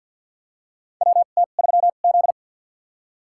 Morse code, heard as a keyed tone, about 52 wpm, spaced as for 36 wpm, 700 Hz. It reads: WT3Z
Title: USA Call Signs - 2x Character Spacing - 50wpm